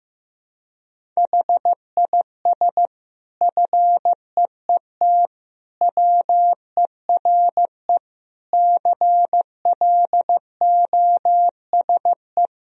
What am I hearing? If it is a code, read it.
Morse code, heard as a keyed tone, about 15 wpm, 700 Hz. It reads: HIS FEET WERE CLOSE
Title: Sentences from Top 1000 Words - Review Part 1 - 15wpm